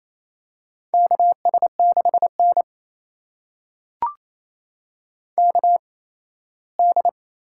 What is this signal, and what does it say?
Morse code, heard as a keyed tone, about 28 wpm, spaced as for 17 wpm, 700 Hz. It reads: KS6N KD
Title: USA Call Signs - 8x Character Spacing - 28wpm